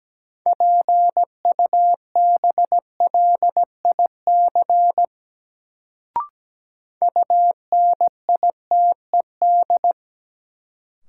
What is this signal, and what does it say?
Morse code, heard as a keyed tone, about 17 wpm, 700 Hz. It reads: PUBLIC UNITED
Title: Sentences from Top 600 Words - New Word Drill - 17wpm